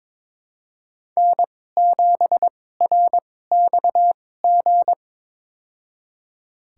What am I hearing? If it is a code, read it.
Morse code, heard as a keyed tone, about 22 wpm, spaced as for 16 wpm, 700 Hz. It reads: N7RXG
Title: USA Call Signs - 5 Characters - 2x Character Spacing - 22wpm